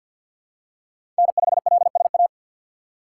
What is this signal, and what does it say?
Morse code, heard as a keyed tone, about 47 wpm, 700 Hz. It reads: N5LSA
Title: USA Call Signs - 6x Character Spacing - 50wpm